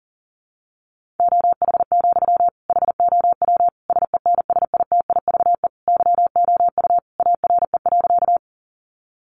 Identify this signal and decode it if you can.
Morse code, heard as a keyed tone, about 37 wpm, 700 Hz. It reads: OH, HOW SENSITIVE YOU ARE.
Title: Sentences from Top 1400 Words 40wpm